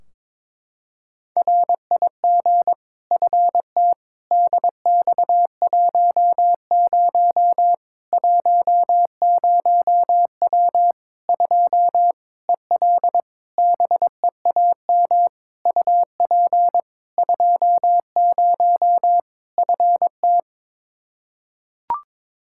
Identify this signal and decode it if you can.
Morse code, heard as a keyed tone, about 22 wpm, 700 Hz. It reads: RIG FT DX10 10W 2 EL BEAM UP 20 FT
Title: Practice QSO - Exchange 2 - Info - 22wpm